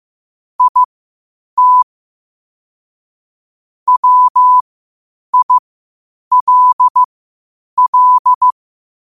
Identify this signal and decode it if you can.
Morse code, heard as a keyed tone, about 15 wpm, spaced as for 8 wpm, 1000 Hz. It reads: IT WILL
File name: qotdfiles / 8wpm